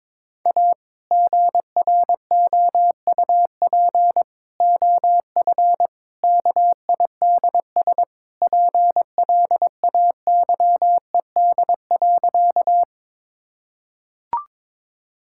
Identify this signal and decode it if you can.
Morse code, heard as a keyed tone, about 22 wpm, 700 Hz. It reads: A GROUP OF KIDS PLAYED.
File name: practice / 22wpm / Sentences with 3-Word Phrases - 22wpm